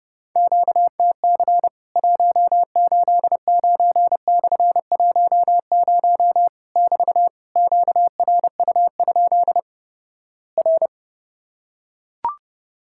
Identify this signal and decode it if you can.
Morse code, heard as a keyed tone, about 31 wpm, 700 Hz. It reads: QTC 189/10 <BT> QRU? R
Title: Worked All Europe DX Contest - QTC Exchange - 30wpm